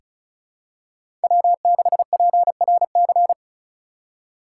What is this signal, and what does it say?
Morse code, heard as a keyed tone, about 36 wpm, 700 Hz. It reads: W6PRC
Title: USA Call Signs - 5 Characters - 4x Character Spacing - 35wpm